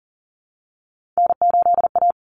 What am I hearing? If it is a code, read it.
Morse code, heard as a keyed tone, about 40 wpm, 700 Hz. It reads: N8A
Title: Common Cut Numbers - 3-Digit 40wpm